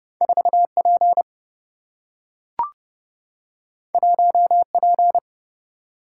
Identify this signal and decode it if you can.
Morse code, heard as a keyed tone, about 30 wpm, 700 Hz. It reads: 4P 1P
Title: Single-Letter Single-Number 30wpm